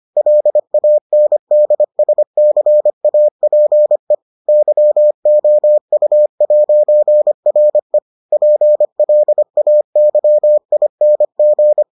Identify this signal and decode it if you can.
Morse code, heard as a keyed tone, about 25 wpm, 600 Hz. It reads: LANDSCAPE YOU'RE PLAYING